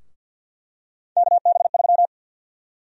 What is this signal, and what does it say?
Morse code, heard as a keyed tone, about 51 wpm, 700 Hz. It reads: KB3
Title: USA Call Sign Prefixes - 50wpm